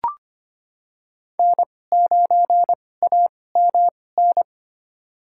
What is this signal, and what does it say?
Morse code, heard as a keyed tone, about 25 wpm, spaced as for 18 wpm, 700 Hz. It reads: N9AMN
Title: USA Call Signs - 5 Characters - 2x Character Spacing - 25wpm